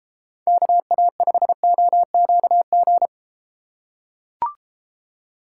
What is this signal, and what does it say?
Morse code, heard as a keyed tone, about 31 wpm, 700 Hz. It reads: KA5OQG